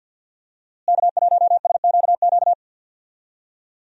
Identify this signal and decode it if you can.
Morse code, heard as a keyed tone, about 52 wpm, 700 Hz. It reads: K1SQQ
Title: USA Call Signs - 5 Characters - Repeated 3x - 50wpm